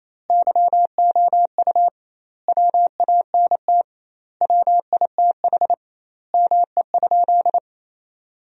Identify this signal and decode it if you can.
Morse code, heard as a keyed tone, about 28 wpm, 700 Hz. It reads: YOU WANT WITH ME?
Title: 2x Word-Spacing - Sentences from Top 200 Words - 28wpm